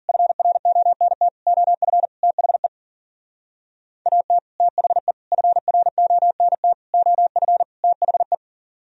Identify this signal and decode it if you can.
Morse code, heard as a keyed tone, about 48 wpm, 700 Hz. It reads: FRONT OF THE AT THE FRONT OF THE
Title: Speed-Racing - Sets of 5 Words 40wpm